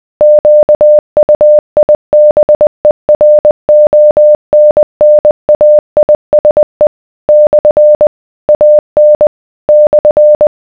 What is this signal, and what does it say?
Morse code, heard as a keyed tone, about 20 wpm, 600 Hz. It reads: QUIBERONNAISE / AN /